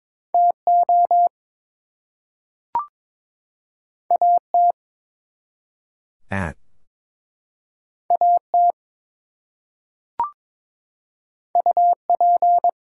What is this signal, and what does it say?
Morse code, heard as a keyed tone, about 22 wpm, 700 Hz. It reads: TO AT AT UP
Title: Two Letter Words - 22wpm